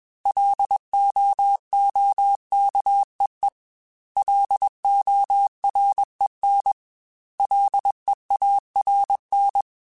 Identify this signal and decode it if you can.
Morse code, heard as a keyed tone, about 21 wpm, 800 Hz. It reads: LOOKEE LOREN LEARN
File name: L Runs / L Runs mp3s / L Runs @20